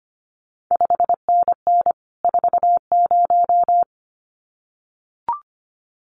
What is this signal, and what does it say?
Morse code, heard as a keyed tone, about 25 wpm, 700 Hz. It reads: 5NN 40
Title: CQ WW DX Contest Exchange Encore - 25wpm